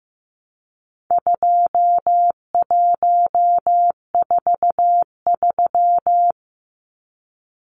Morse code, heard as a keyed tone, about 15 wpm, 700 Hz. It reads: 2143
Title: Numbers - 4-Digit 15wpm